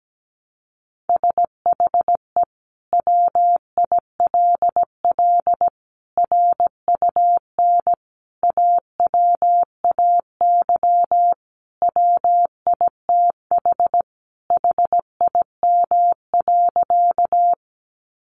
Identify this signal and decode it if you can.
Morse code, heard as a keyed tone, about 17 wpm, 700 Hz. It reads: SHE WILL RUN AWAY WITH HIM.
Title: Sentences from Top 600 Words 17wpm